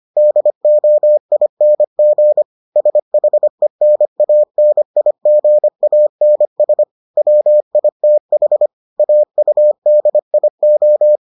Morse code, heard as a keyed tone, about 25 wpm, 600 Hz. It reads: DOING SHENANIGANS WITH AUDIO